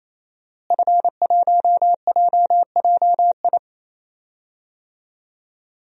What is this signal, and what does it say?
Morse code, heard as a keyed tone, about 27 wpm, 700 Hz. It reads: F1JJI
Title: DXCC-100 Call Signs - 5 Characters - Repeated 3x - 28wpm